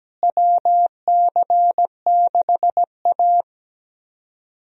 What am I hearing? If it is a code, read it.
Morse code, heard as a keyed tone, about 17 wpm, 700 Hz. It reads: WC6A
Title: USA Call Signs - 4 Characters - Repeated 3x - 17wpm